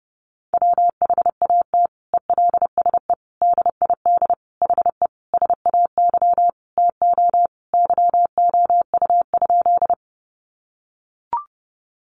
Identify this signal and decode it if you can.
Morse code, heard as a keyed tone, about 30 wpm, 700 Hz. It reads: WHAT ELSE DID HE SAY TO YOU?
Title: Sentences from Top 800 Words 30wpm